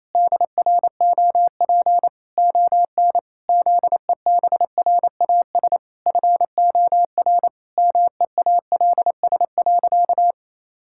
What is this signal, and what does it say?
Morse code, heard as a keyed tone, about 27 wpm, 700 Hz. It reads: DROP ON ZEBRAS FOR MEALS.
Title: Pangram Challenge v2 - 28wpm